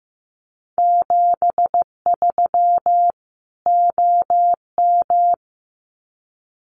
Morse code, heard as a keyed tone, about 15 wpm, 700 Hz. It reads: 73 OM
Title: Common QSO Phrases 15wpm